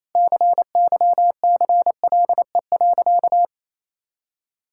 Morse code, heard as a keyed tone, about 28 wpm, 700 Hz. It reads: CYCLE.
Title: Sentences from Top 1000 Words - Review Part 6 - 28wpm